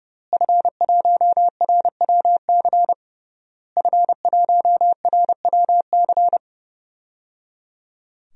Morse code, heard as a keyed tone, about 30 wpm, 700 Hz. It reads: F1RWC F1RWC
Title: DXCC-100 Call Signs - 5 Characters - Repeated 3x - 30wpm